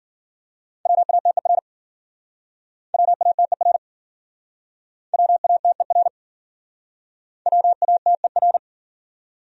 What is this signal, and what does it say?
Morse code, heard as a keyed tone, about 51 wpm, 700 Hz. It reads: WATER WATER WATER WATER